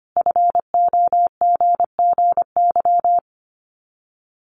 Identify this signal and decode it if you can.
Morse code, heard as a keyed tone, about 25 wpm, 700 Hz. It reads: FOGGY
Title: Common QSO Phrases 25wpm